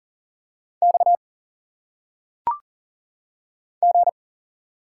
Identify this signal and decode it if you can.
Morse code, heard as a keyed tone, about 40 wpm, 700 Hz. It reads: X G
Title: Single Letters - Phonetic Pronunciation 40wpm